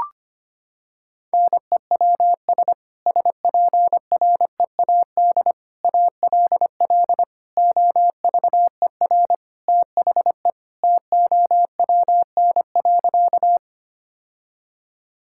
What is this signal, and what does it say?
Morse code, heard as a keyed tone, about 25 wpm, 700 Hz. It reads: NEWS SPREAD ALL OVER THE TOWN.
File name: practice / 25wpm / Sentences with 3-Word Phrases - 25wpm